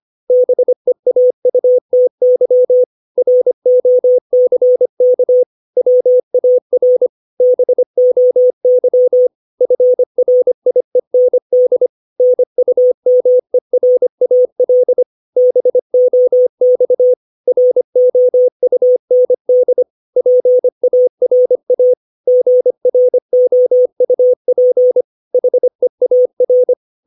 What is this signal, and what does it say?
Morse code, heard as a keyed tone, about 25 wpm, 500 Hz. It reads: BEAUTY ROCK WAR BOY FRIEND NUMERAL BOX ROUND PARA GROUP HEAR
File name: 632 words / 632 Words 25WPM